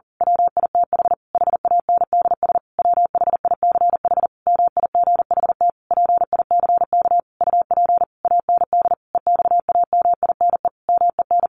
Morse code, heard as a keyed tone, about 40 wpm, 700 Hz. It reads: WITH HANDS WHICH MIGHT PICK UP AND EXAMINE MEN
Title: A Sound Of Thunder 40wpm